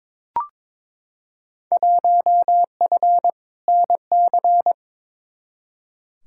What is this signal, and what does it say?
Morse code, heard as a keyed tone, about 21 wpm, 700 Hz. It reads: 1F NC
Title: Field Day Exchanges - 22wpm